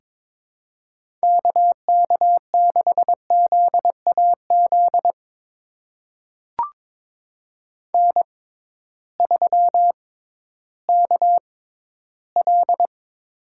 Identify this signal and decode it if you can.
Morse code, heard as a keyed tone, about 22 wpm, spaced as for 13 wpm, 700 Hz. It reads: KK6ZAZ N3KL
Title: USA Call Signs - 6x Character Spacing - 22wpm